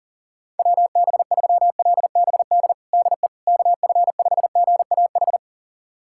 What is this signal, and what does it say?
Morse code, heard as a keyed tone, about 40 wpm, 700 Hz. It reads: WB3LBD DE KF5GAH
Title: USA Callsign DE Callsign 40wpm